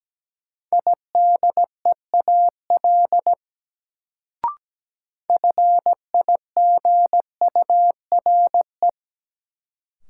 Morse code, heard as a keyed tone, about 17 wpm, 700 Hz. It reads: IDEAL FIGURE
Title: Sentences from Top 1400 Words - New Word Drill Alt - 17wpm